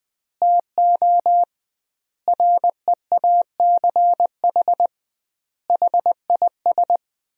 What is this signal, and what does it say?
Morse code, heard as a keyed tone, about 20 wpm, 700 Hz. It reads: TO REACH HIS
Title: 2x Word-Spacing - Sentences from Top 400 Words - 20wpm